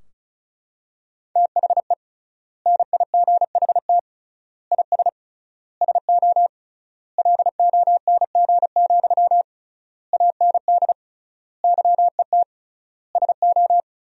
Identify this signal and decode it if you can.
Morse code, heard as a keyed tone, about 36 wpm, 700 Hz. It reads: THE NIGHT IS SO LONG, AND YET SO
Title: 3x Word-Spacing - Sentences from Top 500 Words - 35wpm